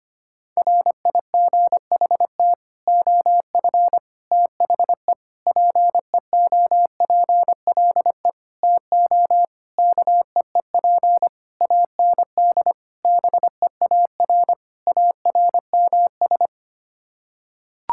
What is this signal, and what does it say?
Morse code, heard as a keyed tone, about 25 wpm, 700 Hz. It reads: RIGHT OF THE PEOPLE TO KEEP AND BEAR ARMS
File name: Sets of 9 Words - 25wpm